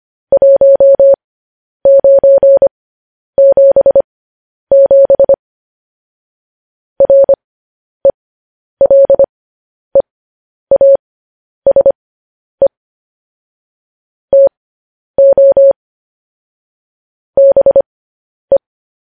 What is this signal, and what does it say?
Morse code, heard as a keyed tone, about 25 wpm, spaced as for 10 wpm, 550 Hz. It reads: 1977 RELEASE TO BE
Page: news headlines in morse code at 10 wpm